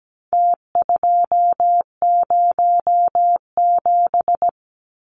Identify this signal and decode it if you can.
Morse code, heard as a keyed tone, about 17 wpm, 700 Hz. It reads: T207
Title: Numbers - Contest Style - 4-Digit 17wpm